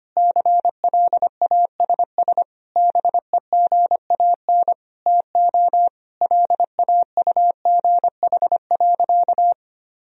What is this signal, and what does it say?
Morse code, heard as a keyed tone, about 25 wpm, 700 Hz. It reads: CLASS BEGAN TO LAUGH.
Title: Sentences from Top 1000 Words - Review Part 2 - 25wpm